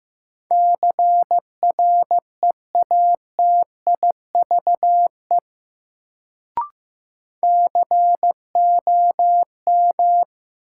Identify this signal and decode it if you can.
Morse code, heard as a keyed tone, about 15 wpm, 700 Hz. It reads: CREATIVE COM
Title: Sentences from Top 1300 Words - New Word Drill Alt - 15wpm